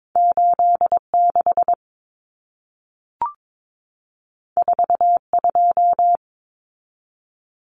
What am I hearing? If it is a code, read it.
Morse code, heard as a keyed tone, about 22 wpm, 700 Hz. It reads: 86 42
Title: Common Cut Numbers - 2-Digit 22wpm